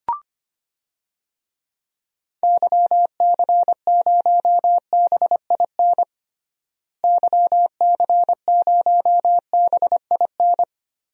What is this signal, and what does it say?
Morse code, heard as a keyed tone, about 25 wpm, 700 Hz. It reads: YC0BIN YC0BIN
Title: DXCC-100 Call Signs - 6 Characters - Repeated 3x - 25wpm